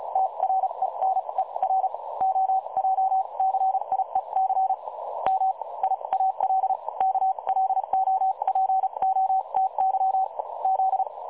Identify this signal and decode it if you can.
Morse code, heard as a keyed tone, about 36 wpm, spaced as for 29 wpm, 760 Hz. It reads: TRENER OJCIEC MIAL KLOPOTY Z
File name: dlazaawansowanych / nagraniazpasma / websdr recording 3565.0kHz